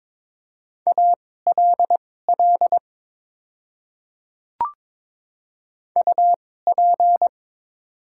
Morse code, heard as a keyed tone, about 22 wpm, spaced as for 16 wpm, 700 Hz. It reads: ALL UP
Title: 2x Character Spacing - Top 100 Words - 22wpm